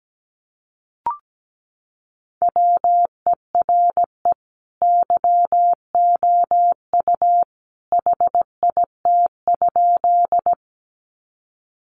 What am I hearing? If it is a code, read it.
Morse code, heard as a keyed tone, about 17 wpm, 700 Hz. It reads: WERE YOU HIT?
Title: Sentences from Top 1200 Words 17wpm